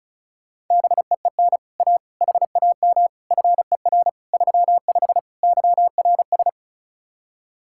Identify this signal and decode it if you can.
Morse code, heard as a keyed tone, about 45 wpm, 700 Hz. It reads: BEEN A HAM FER 35 YRS